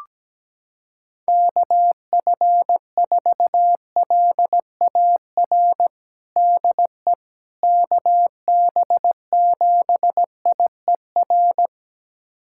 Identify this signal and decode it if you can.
Morse code, heard as a keyed tone, about 17 wpm, 700 Hz. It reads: KF4LAR DE KB7IER
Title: USA Callsign DE Callsign - 6 Characters - 17wpm